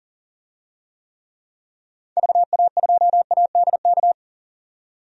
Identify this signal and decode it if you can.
Morse code, heard as a keyed tone, about 40 wpm, 700 Hz. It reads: VA2ADK
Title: DXCC-100 Call Signs - 6 Characters - 40wpm